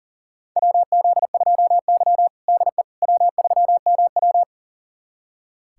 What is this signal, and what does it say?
Morse code, heard as a keyed tone, about 40 wpm, 700 Hz. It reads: WZ2Y DE W3MW
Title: USA Callsign DE Callsign - 4 Characters - 40wpm